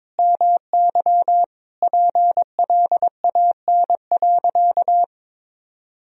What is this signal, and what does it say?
Morse code, heard as a keyed tone, about 22 wpm, 700 Hz. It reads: MY PLAN.
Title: Sentences from Top 1000 Words - Review Part 2 - 22wpm